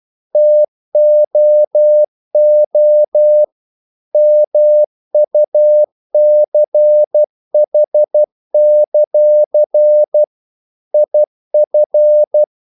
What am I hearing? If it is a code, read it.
Morse code, heard as a keyed tone, about 12 wpm, 600 Hz. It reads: TOO MUCH; IF
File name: CW / Practice / If12